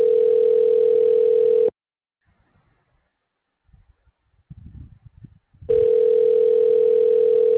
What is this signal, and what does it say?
Ringback tone (North America), 440 + 480 Hz.